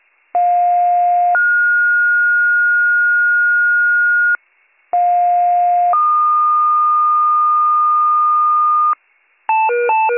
Two-tone page, heard as a one-second tone, then a three-second tone, about 710 then 1430 Hz, about 710 then 1220 Hz, with noise. It is heard from 0.3 to 8.9 s.